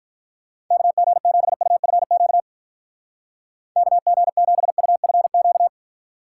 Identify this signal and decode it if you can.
Morse code, heard as a keyed tone, about 44 wpm, 700 Hz. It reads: KG7UFQ KG7UFQ
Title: Speed Racing - USA 2x3 Call Signs 35wpm